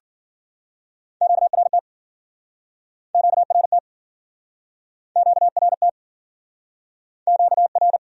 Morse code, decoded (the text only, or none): QRT QRT QRT QR